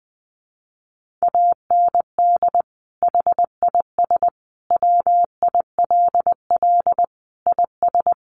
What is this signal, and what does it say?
Morse code, heard as a keyed tone, about 20 wpm, 700 Hz. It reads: AND HIS WILL IS
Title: Sentences from Top 1500 Words 20wpm